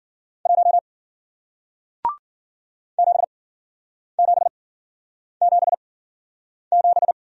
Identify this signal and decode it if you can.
Morse code, heard as a keyed tone, about 53 wpm, spaced as for 9 wpm, 700 Hz. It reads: 1 7777